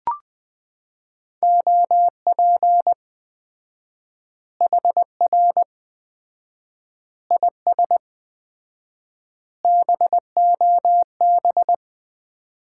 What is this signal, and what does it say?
Morse code, heard as a keyed tone, about 20 wpm, 700 Hz. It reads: OP HR IS BOB
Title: Common QSO Phrases - 4x Word-Spacing - 20wpm